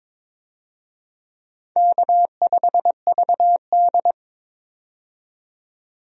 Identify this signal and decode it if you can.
Morse code, heard as a keyed tone, about 22 wpm, 700 Hz. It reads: K5VD
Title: USA Call Signs - 4 Characters - 22wpm